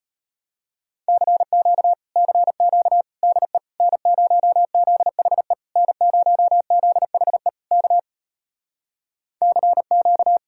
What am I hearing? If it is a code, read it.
Morse code, heard as a keyed tone, about 33 wpm, 700 Hz. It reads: CQ CQ DE N0ZHE N0ZHE K CQ